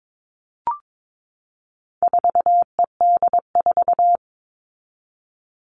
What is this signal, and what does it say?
Morse code, heard as a keyed tone, about 22 wpm, 700 Hz. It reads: 4ED4
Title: All Cut Numbers - 4-Digit 22wpm